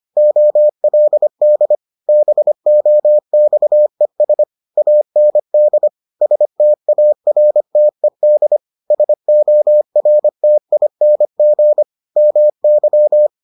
Morse code, heard as a keyed tone, about 25 wpm, 600 Hz. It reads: OLD BOXES AND STARTED SORTING MY